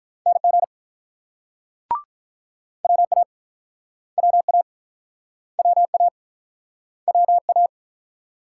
Morse code, heard as a keyed tone, about 48 wpm, 700 Hz. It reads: NG WA WA WA WA